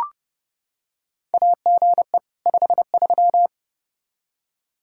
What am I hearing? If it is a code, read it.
Morse code, heard as a keyed tone, about 30 wpm, 700 Hz. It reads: AGE 53
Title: K1USN SST QSO Practice 30wpm